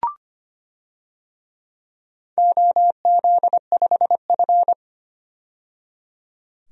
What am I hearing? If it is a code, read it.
Morse code, heard as a keyed tone, about 25 wpm, 700 Hz. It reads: OZ5F
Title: DXCC-100 Call Signs - 4 Characters - 25wpm